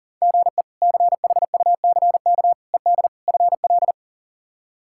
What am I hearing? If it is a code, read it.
Morse code, heard as a keyed tone, about 40 wpm, 700 Hz. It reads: GE CHUCK ED FL
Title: K1USN SST QSO Practice - 40wpm